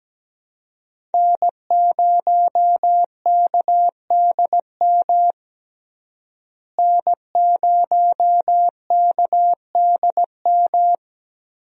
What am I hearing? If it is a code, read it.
Morse code, heard as a keyed tone, about 17 wpm, 700 Hz. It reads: N0KDM N0KDM
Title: USA Call Signs - 5 Characters - Repeated 2x - 17wpm